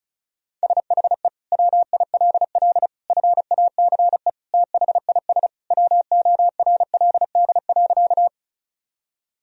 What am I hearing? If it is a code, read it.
Morse code, heard as a keyed tone, about 36 wpm, 700 Hz. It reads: SHE WILL FACE THIS WORLD.
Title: Sentences from Top 1000 Words - Review Part 2 - 35wpm